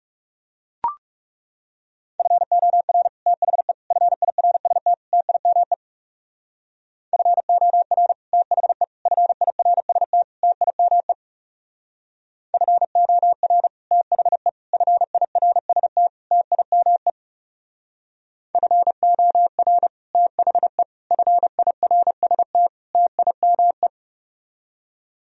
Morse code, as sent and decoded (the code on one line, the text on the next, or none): ..-. --- .-. / - .... / ..-. .. .-. ... - / - .. -- . / ..-. --- .-. / - .... / ..-. .. .-. ... - / - .. -- . / ..-. --- .-. / - .... / ..-. .. .-. ... - / - .. -- . / ..-. --- .-. / - .... / ..-. .. .-. ... - / - .. -- .
FOR THE FIRST TIME FOR THE FIRST TIME FOR THE FIRST TIME FOR THE FIRST TIME